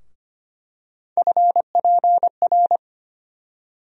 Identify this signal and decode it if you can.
Morse code, heard as a keyed tone, about 25 wpm, 700 Hz. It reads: FPR
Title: USA Call Sign Postfixes - 3 Letters 25wpm